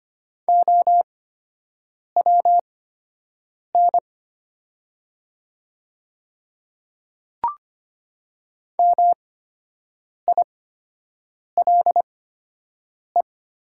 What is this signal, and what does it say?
Morse code, heard as a keyed tone, about 25 wpm, spaced as for 7 wpm, 700 Hz. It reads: OWN MILE
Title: 8x Character Spacing - Top 300 Words - 25wpm